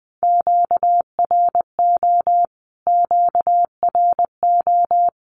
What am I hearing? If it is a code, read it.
Morse code, heard as a keyed tone, about 20 wpm, 700 Hz. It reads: QRO QRO
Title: Q-Codes Repeated 3x 20wpm